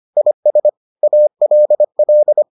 Morse code, heard as a keyed tone, about 25 wpm, 600 Hz. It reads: IS ALL